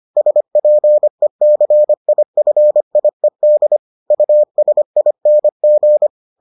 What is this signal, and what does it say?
Morse code, heard as a keyed tone, about 25 wpm, 600 Hz. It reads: SPECIFIED USING